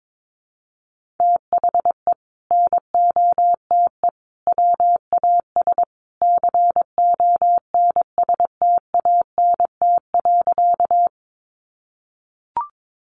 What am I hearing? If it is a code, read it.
Morse code, heard as a keyed tone, about 22 wpm, 700 Hz. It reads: THE NOTE WAS CONSTANT.
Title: Sentences from Top 1500 Words 22wpm